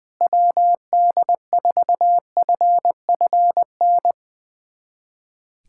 Morse code, heard as a keyed tone, about 20 wpm, 700 Hz. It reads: WD4FFN
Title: USA Callsign DE Callsign - 6 Characters - 20wpm